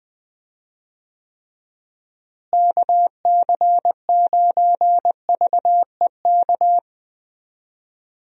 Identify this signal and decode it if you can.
Morse code, heard as a keyed tone, about 20 wpm, 700 Hz. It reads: KC9VEK
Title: USA Call Signs - 6 Characters - Repeated 3x - 20wpm